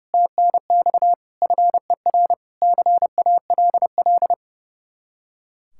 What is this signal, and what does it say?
Morse code, heard as a keyed tone, about 36 wpm, 700 Hz. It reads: TNX FER CALL
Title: Speed-Racing Common QSO Phrases - 30wpm